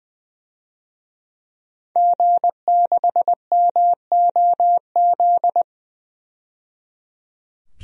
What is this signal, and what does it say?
Morse code, heard as a keyed tone, about 20 wpm, 700 Hz. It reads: G6MOZ
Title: DXCC-100 Call Signs - 5 Characters - 20wpm